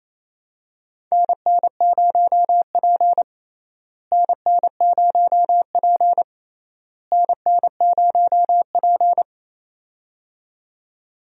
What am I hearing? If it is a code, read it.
Morse code, heard as a keyed tone, about 28 wpm, 700 Hz. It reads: NN0P NN0P NN0P
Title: USA Call Signs - 4 Characters - Repeated 3x - 28wpm